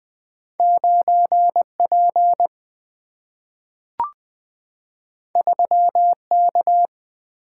Morse code, heard as a keyed tone, about 20 wpm, 700 Hz. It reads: 9P 3K
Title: Single-Letter Single-Number 20wpm